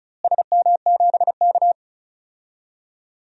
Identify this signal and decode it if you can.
Morse code, heard as a keyed tone, about 34 wpm, 700 Hz. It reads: SM7K
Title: DXCC-100 Call Signs - 4 Characters - Repeated 3x - 35wpm